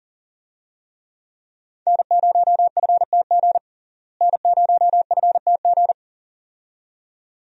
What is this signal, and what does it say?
Morse code, heard as a keyed tone, about 40 wpm, 700 Hz. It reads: N0FTG N0FTG
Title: USA Call Signs - 5 Characters - Repeated 2x - 40wpm